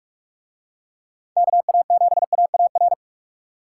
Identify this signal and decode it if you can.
Morse code, heard as a keyed tone, about 45 wpm, 700 Hz. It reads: KA7AAR